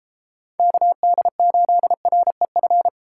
Morse code, heard as a keyed tone, about 39 wpm, 700 Hz. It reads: KD8REF